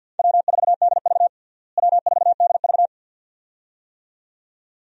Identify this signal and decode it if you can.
Morse code, heard as a keyed tone, about 52 wpm, 700 Hz. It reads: W4DV W4DV
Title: USA Call Signs - 4 Characters - Repeated 2x - 50wpm